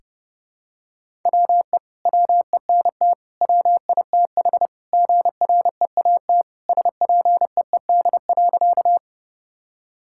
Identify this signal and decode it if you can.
Morse code, heard as a keyed tone, about 32 wpm, 700 Hz. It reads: WE WENT WITH GREAT SPEED.
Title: Sentences from Top 1000 Words - Review Part 5 - 30wpm